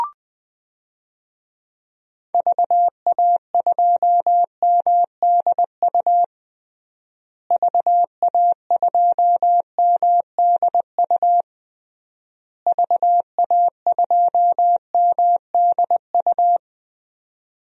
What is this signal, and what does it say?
Morse code, heard as a keyed tone, about 20 wpm, 700 Hz. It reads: VA2MDU VA2MDU VA2MDU
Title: DXCC-100 Call Signs - 6 Characters - Repeated 3x - 20wpm